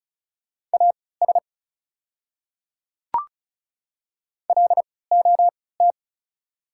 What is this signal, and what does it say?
Morse code, heard as a keyed tone, about 34 wpm, spaced as for 20 wpm, 690 Hz. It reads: AS LOT
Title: 3x Character Spacing - Top 500 Words - 35wpm